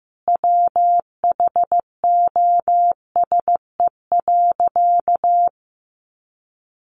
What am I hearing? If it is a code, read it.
Morse code, heard as a keyed tone, about 15 wpm, 700 Hz. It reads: WHOSE.
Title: Sentences from Top 1200 Words 15wpm